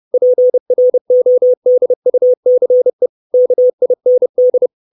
Morse code, heard as a keyed tone, about 30 wpm, 500 Hz. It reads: PRODUCE KIND
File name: 632 Words 30WPM